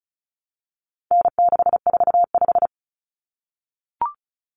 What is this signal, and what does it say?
Morse code, heard as a keyed tone, about 34 wpm, 700 Hz. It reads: N645
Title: Common Cut Numbers - 4-Digit 35wpm